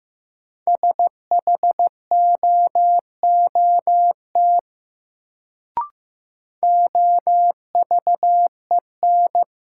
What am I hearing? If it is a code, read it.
Morse code, heard as a keyed tone, about 15 wpm, 700 Hz. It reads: SHOOT OVEN
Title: Sentences from Top 1200 Words - New Word Drill Alt - 15wpm